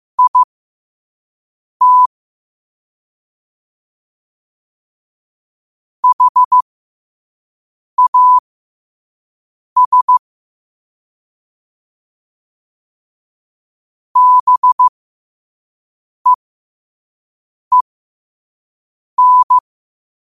Morse code, heard as a keyed tone, about 15 wpm, spaced as for 5 wpm, 1000 Hz. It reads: IT HAS BEEN
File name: qotdfiles / 5wpm